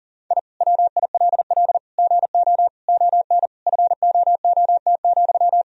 Morse code, heard as a keyed tone, about 40 wpm, 700 Hz. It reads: I WILL GO ON FOOT,
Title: Sentences from Top 1000 Words - Review Part 3 - 40wpm